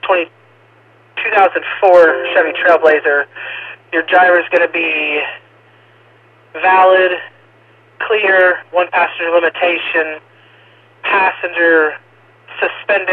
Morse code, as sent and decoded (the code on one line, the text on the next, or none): none